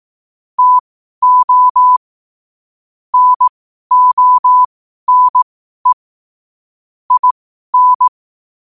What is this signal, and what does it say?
Morse code, heard as a keyed tone, about 18 wpm, spaced as for 12 wpm, 1000 Hz. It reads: TO NONE IN
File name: qotdfiles / 12wpm